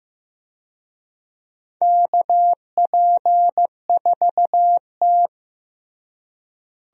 Morse code, heard as a keyed tone, about 15 wpm, 700 Hz. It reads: KP4T
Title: DXCC-100 Call Signs - 4 Characters - Repeated 3x - 15wpm